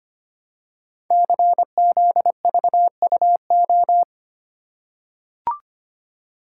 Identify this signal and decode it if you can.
Morse code, heard as a keyed tone, about 25 wpm, 700 Hz. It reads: CZVUO
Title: Code Groups - 5 Letter:Numbers - 25wpm